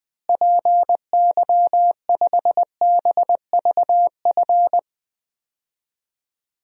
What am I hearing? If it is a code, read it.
Morse code, heard as a keyed tone, about 20 wpm, 700 Hz. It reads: PY5BVF
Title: DXCC-100 Call Signs - 6 Characters - Repeated 3x - 20wpm